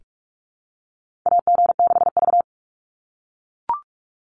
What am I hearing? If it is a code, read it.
Morse code, heard as a keyed tone, about 44 wpm, 700 Hz. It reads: AG6V
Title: Call Signs - 45wpm